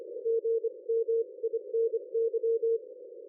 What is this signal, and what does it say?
Morse code, heard as a keyed tone, about 25 wpm, 460 Hz. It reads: GMINY
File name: dlazaawansowanych / nagraniazpasma / ExpertSDR2 line out 23.01.2020 14.12.00